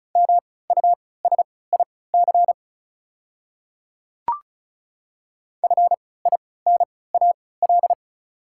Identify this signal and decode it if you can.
Morse code, heard as a keyed tone, about 36 wpm, spaced as for 20 wpm, 690 Hz. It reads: MUSIC FINAL
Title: 3x Character Spacing - Top 500 Words - 35wpm